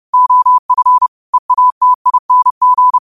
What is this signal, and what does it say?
Morse code, heard as a keyed tone, about 30 wpm, 1000 Hz. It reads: OF EATING